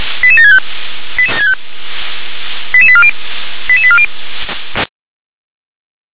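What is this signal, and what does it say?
Five-tone selective call (ZVEI), heard as quick runs of five tones, tones of about 70 ms. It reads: 88765 88840 88840
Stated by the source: ZVEI five-tone selective call received on SDR.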